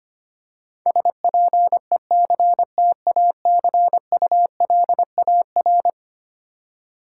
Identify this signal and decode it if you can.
Morse code, heard as a keyed tone, about 25 wpm, 700 Hz. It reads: SPECTACULAR
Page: morse-code-ninja-practice-25wpm